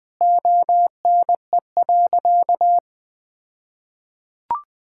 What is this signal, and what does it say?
Morse code, heard as a keyed tone, about 21 wpm, 700 Hz. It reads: ONE.
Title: All-Age Appropriate Jokes - 20wpm